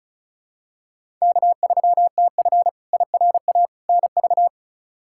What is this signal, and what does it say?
Morse code, heard as a keyed tone, about 36 wpm, 700 Hz. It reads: K3TF IRA NV
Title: K1USN SST QSO Practice - 35wpm